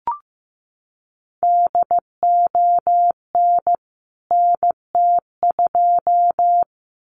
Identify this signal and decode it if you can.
Morse code, heard as a keyed tone, about 15 wpm, 700 Hz. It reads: DON NT2
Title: CWT Exchange 15wpm